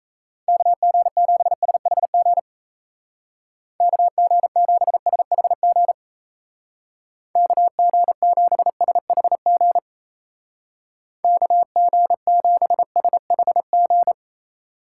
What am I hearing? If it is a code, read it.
Morse code, heard as a keyed tone, about 34 wpm, 700 Hz. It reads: KG7SHG KG7SHG KG7SHG KG7SHG
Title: Speed Racing - USA 2x3 Call Signs 28wpm